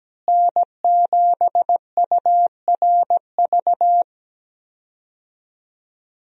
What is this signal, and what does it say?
Morse code, heard as a keyed tone, about 17 wpm, 700 Hz. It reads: N7URV
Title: USA Call Signs - 5 Characters - Repeated 3x - 17wpm